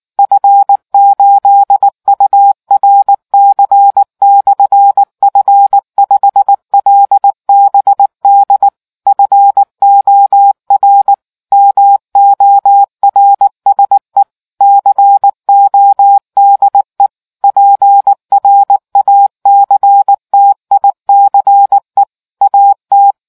Morse code, heard as a keyed tone, about 19 wpm, 800 Hz. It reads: F8URC/F5LBD FOR MORSE CODE PRACTICE AT